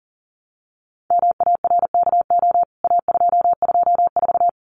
Morse code, heard as a keyed tone, about 40 wpm, 700 Hz. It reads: MARKO A224
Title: CWT Exchange 40wpm